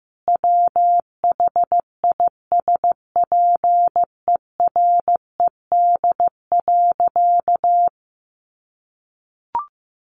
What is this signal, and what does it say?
Morse code, heard as a keyed tone, about 14 wpm, 700 Hz. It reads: WHISPERED.
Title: A Sound Of Thunder 15wpm